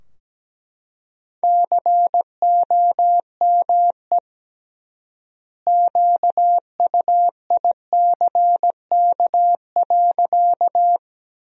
Morse code, heard as a keyed tone, about 17 wpm, 700 Hz. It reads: COME QUICK.
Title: 3x Word-Spacing - Sentences from Top 500 Words - 17wpm